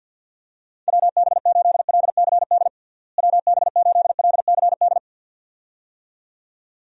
Morse code, heard as a keyed tone, about 51 wpm, 700 Hz. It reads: WB8LCD WB8LCD
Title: USA Call Signs - 6 Characters - Repeated 2x - 50wpm